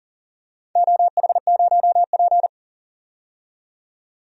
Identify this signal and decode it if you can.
Morse code, heard as a keyed tone, about 40 wpm, 700 Hz. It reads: OH0P
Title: DXCC-100 Call Signs - 4 Characters - 40wpm